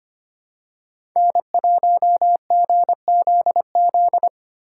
Morse code, heard as a keyed tone, about 25 wpm, 700 Hz. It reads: N1GZZ
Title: USA Call Signs - 5 Characters - 4x Character Spacing - 25wpm